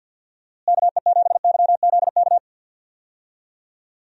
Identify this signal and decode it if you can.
Morse code, heard as a keyed tone, about 53 wpm, 700 Hz. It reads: KE7YZK
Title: USA Call Signs - 6 Characters - Repeated 2x - 50wpm